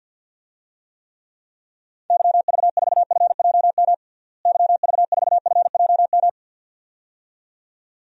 Morse code, heard as a keyed tone, about 52 wpm, 700 Hz. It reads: YV4FJM YV4FJM
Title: DXCC-100 Call Signs - 6 Characters - Repeated 2x - 50wpm